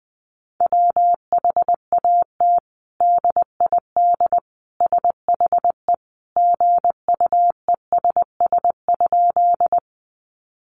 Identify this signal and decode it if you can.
Morse code, heard as a keyed tone, about 20 wpm, 700 Hz. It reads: WHAT DID SHE GUESS?
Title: Sentences from Top 1500 Words 20wpm